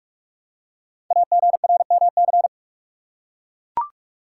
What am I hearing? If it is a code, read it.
Morse code, heard as a keyed tone, about 44 wpm, 690 Hz. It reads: AGRMC
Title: Code Groups - 5 Letters - 45wpm